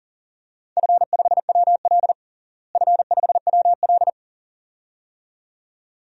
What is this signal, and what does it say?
Morse code, heard as a keyed tone, about 40 wpm, 700 Hz. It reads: F5WL F5WL
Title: DXCC-100 Call Signs - 4 Characters - Repeated 2x - 40wpm